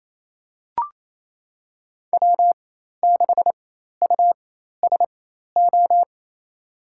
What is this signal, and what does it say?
Morse code, heard as a keyed tone, about 27 wpm, spaced as for 13 wpm, 700 Hz. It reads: W6USO
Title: USA Call Signs - 5 Characters - 4x Character Spacing - 28wpm